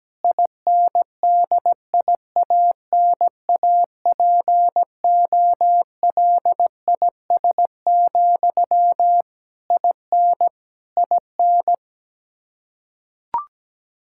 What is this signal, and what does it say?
Morse code, heard as a keyed tone, about 17 wpm, 700 Hz. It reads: INDIANAPOLIS, IN IN
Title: Practice QSO - Exchange 1 - Beginning - 17wpm